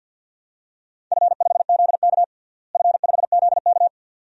Morse code, heard as a keyed tone, about 50 wpm, 700 Hz. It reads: F5ZK F5ZK
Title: DXCC-100 Call Signs - 4 Characters - Repeated 2x - 50wpm